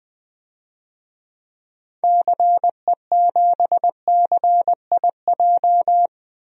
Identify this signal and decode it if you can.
Morse code, heard as a keyed tone, about 20 wpm, 700 Hz. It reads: CE7CIJ
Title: DXCC-100 Call Signs - 6 Characters - 20wpm